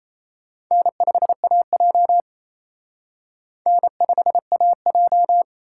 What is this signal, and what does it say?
Morse code, heard as a keyed tone, about 35 wpm, 700 Hz. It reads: N5AJ N5AJ